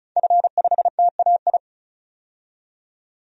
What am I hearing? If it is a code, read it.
Morse code, heard as a keyed tone, about 36 wpm, 700 Hz. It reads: F5TAI
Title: DXCC-100 Call Signs - 5 Characters - 35wpm